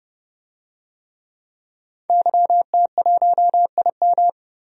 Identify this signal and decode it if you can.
Morse code, heard as a keyed tone, about 30 wpm, 700 Hz. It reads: YT1IM
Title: DXCC-100 Call Signs - 5 Characters - Repeated 3x - 30wpm